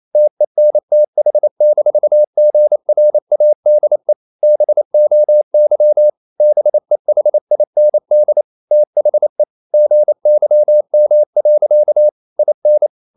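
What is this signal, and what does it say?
Morse code, heard as a keyed tone, about 28 wpm, 600 Hz. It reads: TENTH-GRADE BOY BEHIND THE GYM. IN
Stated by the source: Morse code practice at 28 wpm, text of Ken Follett's novel Never in English, from CW med Gnister.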